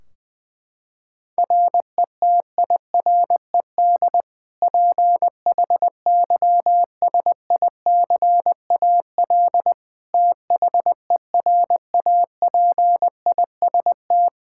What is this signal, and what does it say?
Morse code, heard as a keyed tone, about 20 wpm, 700 Hz. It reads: RETIRED PHYSICAL THERAPIST
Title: K1USN SST QSO Practice 20wpm